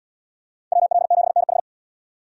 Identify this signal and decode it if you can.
Morse code, heard as a keyed tone, about 78 wpm, 700 Hz. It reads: WL7NH